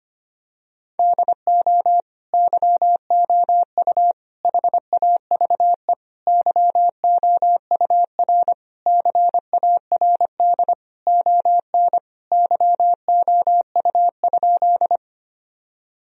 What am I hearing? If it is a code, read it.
Morse code, heard as a keyed tone, about 25 wpm, 700 Hz. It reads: DO YOU HAVE YOUR CARD ON YOU?
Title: Sentences from Top 1000 Words - Review Part 4 - 25wpm